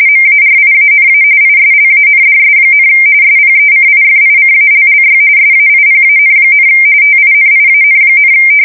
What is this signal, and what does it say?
RTTY, about 45.45 baud. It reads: MARIANA ISLANDS, SPECIFICALLY SAIPAN, TINIAN,
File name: fsk170 / USS-New-Jersey